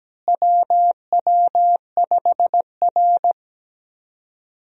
Morse code, heard as a keyed tone, about 17 wpm, 700 Hz. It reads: WW5R
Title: USA Call Signs - 4 Characters - Repeated 2x - 17wpm